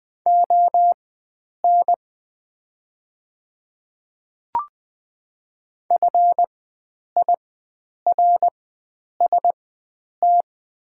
Morse code, heard as a keyed tone, about 20 wpm, spaced as for 9 wpm, 700 Hz. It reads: ON FIRST
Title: 4x Character Spacing - Top 100 Words - 20wpm